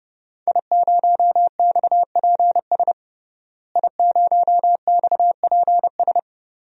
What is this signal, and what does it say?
Morse code, heard as a keyed tone, about 30 wpm, 700 Hz. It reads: I0XPS I0XPS
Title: DXCC-100 Call Signs - 5 Characters - Repeated 2x - 30wpm